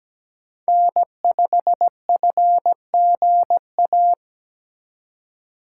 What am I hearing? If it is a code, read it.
Morse code, heard as a keyed tone, about 17 wpm, 700 Hz. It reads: N5FGA
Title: USA Call Signs - 2x Character Spacing - 17wpm